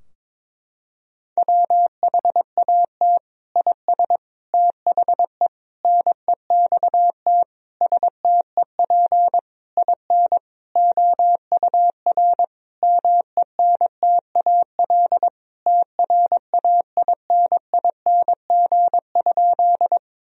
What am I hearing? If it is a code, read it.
Morse code, heard as a keyed tone, about 21 wpm, 700 Hz. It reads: WHAT IS THE NEXT STEP IN OUR MENTAL TRAINING?
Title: Sentences from Top 1000 Words - Review Part 4 - 22wpm